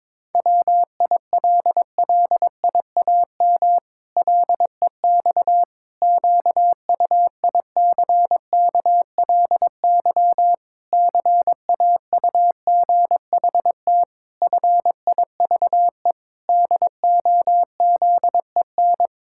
Morse code, heard as a keyed tone, about 22 wpm, 700 Hz. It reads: WILLIAM LEX QUICKLY CAUGHT FIVE DOZEN
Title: Pangram Challenge v2 - 22wpm